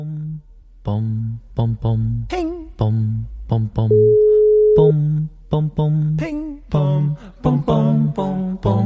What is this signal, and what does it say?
Ringback tone (Europe, 425 Hz), with speech.